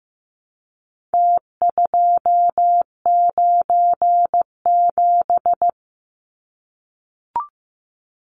Morse code, heard as a keyed tone, about 15 wpm, 700 Hz. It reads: T297
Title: Numbers - Contest Style - 4-Digit 15wpm